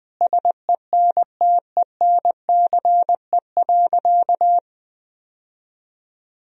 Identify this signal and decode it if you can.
Morse code, heard as a keyed tone, about 20 wpm, 700 Hz. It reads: SENTENCE.
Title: Sentences from Top 1000 Words - Review Part 1 - 20wpm